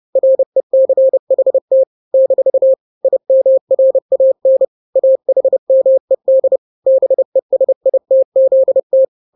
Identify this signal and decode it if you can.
Morse code, heard as a keyed tone, about 30 wpm, 530 Hz. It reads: RECHT - IMRAN AHMED BESITZT
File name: daily / 29wpm